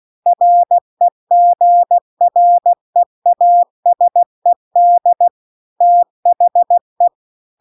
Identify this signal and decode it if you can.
Morse code, heard as a keyed tone, about 16 wpm, 700 Hz. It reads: REGREASED THE
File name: CW-The-Big-Twoheaded-River-2-16WPM